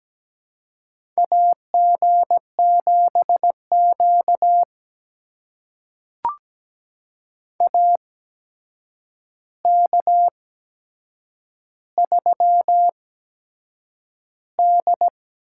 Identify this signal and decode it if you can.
Morse code, heard as a keyed tone, about 17 wpm, spaced as for 7 wpm, 700 Hz. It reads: AG7Q AK3D
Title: USA Call Signs - 4 Characters - 8x Character Spacing - 17wpm